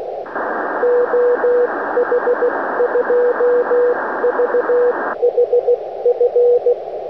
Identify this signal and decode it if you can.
Morse code, heard as a keyed tone, about 16 wpm, 480 Hz, with radio noise. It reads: OH2VHF
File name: oh2vhf-2m